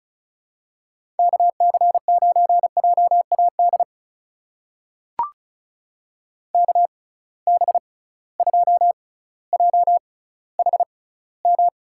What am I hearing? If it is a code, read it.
Morse code, heard as a keyed tone, about 36 wpm, spaced as for 18 wpm, 700 Hz. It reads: KC9JAD KB2JHM
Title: USA Call Signs - 6 Characters - 6x Character Spacing - 35wpm